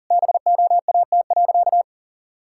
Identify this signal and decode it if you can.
Morse code, heard as a keyed tone, about 40 wpm, 700 Hz. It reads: BOAT.
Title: Sentences from Top 1000 Words - Review Part 5 - 40wpm